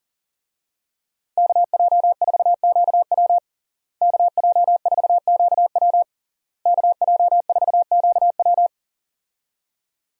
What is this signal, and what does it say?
Morse code, heard as a keyed tone, about 40 wpm, 700 Hz. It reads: KJ4QW KJ4QW KJ4QW
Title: USA Call Signs - 5 Characters - Repeated 3x - 40wpm